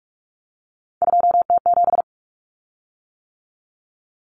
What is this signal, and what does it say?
Morse code, heard as a keyed tone, about 44 wpm, 700 Hz. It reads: 2T7
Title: Common Cut Numbers - 3-Digit 45wpm